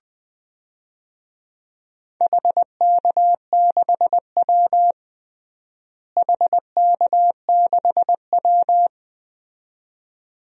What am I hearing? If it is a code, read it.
Morse code, heard as a keyed tone, about 20 wpm, 700 Hz. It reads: HK6W HK6W
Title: DXCC-100 Call Signs - 4 Characters - Repeated 2x - 20wpm